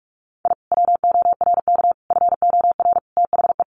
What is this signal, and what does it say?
Morse code, heard as a keyed tone, about 44 wpm, 700 Hz. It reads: I WORK FOR THE
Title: Sentences from Top 600 Words 45wpm